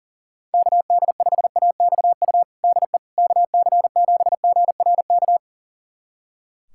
Morse code, heard as a keyed tone, about 40 wpm, 700 Hz. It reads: KD5AXU DE KC7GRK
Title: USA Callsign DE Callsign - 6 Characters - 40wpm